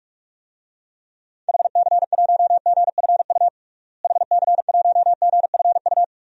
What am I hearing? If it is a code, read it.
Morse code, heard as a keyed tone, about 46 wpm, 700 Hz. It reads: HC1GFU HC1GFU
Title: DXCC-100 Call Signs - 6 Characters - Repeated 3x - 45wpm